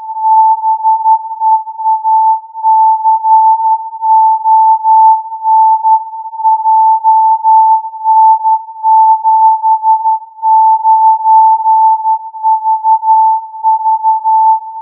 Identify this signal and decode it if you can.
Morse code, heard as a keyed tone, about 12 wpm, 880 Hz. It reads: BEACON JN79VV